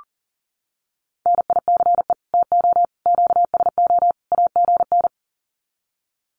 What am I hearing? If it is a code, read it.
Morse code, heard as a keyed tone, about 40 wpm, 700 Hz. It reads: NICE TO QSO AGN
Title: Common QSO Phrases 40wpm